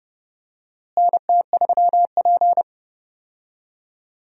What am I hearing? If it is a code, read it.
Morse code, heard as a keyed tone, about 30 wpm, 700 Hz. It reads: NT3P
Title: USA Call Signs - 4 Characters - Repeated 3x - 30wpm